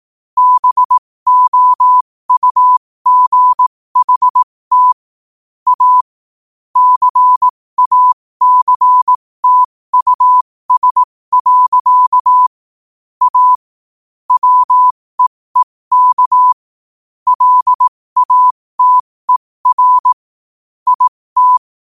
Morse code, heard as a keyed tone, about 18 wpm, spaced as for 15 wpm, 1000 Hz. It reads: BOUGHT A CACTUS. A WEEK LATER IT
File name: qotdfiles / 15wpm